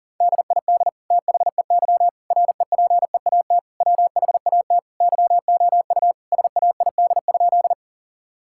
Morse code, heard as a keyed tone, about 40 wpm, 700 Hz. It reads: DID THEY REPEAT WHAT YOU SAID?
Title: Sentences from Top 1600 Words 40wpm